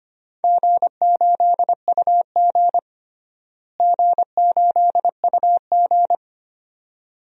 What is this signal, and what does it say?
Morse code, heard as a keyed tone, about 25 wpm, 700 Hz. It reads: G8UG G8UG
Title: DXCC-100 Call Signs - 4 Characters - Repeated 2x - 25wpm